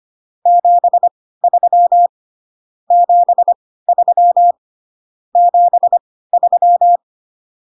Morse code, heard as a keyed tone, about 25 wpm, spaced as for 16 wpm, 700 Hz. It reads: 73 73 73